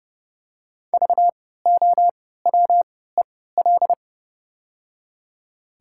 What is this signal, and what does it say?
Morse code, heard as a keyed tone, about 30 wpm, spaced as for 17 wpm, 690 Hz. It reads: VOWEL
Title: 3x Character Spacing - Top 500 Words - 30wpm